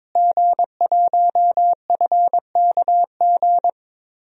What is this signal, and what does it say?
Morse code, heard as a keyed tone, about 22 wpm, 700 Hz. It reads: G1FKG
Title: DXCC-100 Call Signs - 5 Characters - Repeated 2x - 22wpm